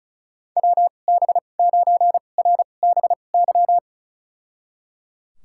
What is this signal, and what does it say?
Morse code, heard as a keyed tone, about 36 wpm, spaced as for 25 wpm, 700 Hz. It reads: WB9RBY
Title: USA Call Signs - 2x Character Spacing - 35wpm